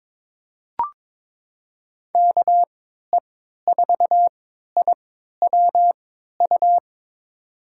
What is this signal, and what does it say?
Morse code, heard as a keyed tone, about 22 wpm, spaced as for 12 wpm, 700 Hz. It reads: KE4IWU
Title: USA Call Signs - 6 Characters - 3x Character Spacing - 22wpm